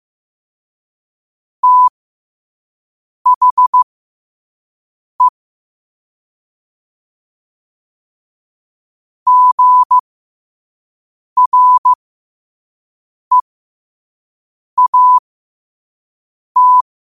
Morse code, heard as a keyed tone, about 15 wpm, spaced as for 5 wpm, 1000 Hz. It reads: THE GREAT
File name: qotdfiles / 5wpm